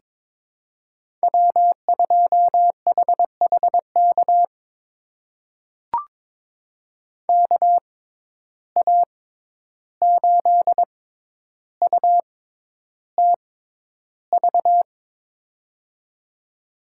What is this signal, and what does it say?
Morse code, heard as a keyed tone, about 21 wpm, spaced as for 11 wpm, 700 Hz. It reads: W2HHK KA8UTV
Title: USA Call Signs - 6x Character Spacing - 22wpm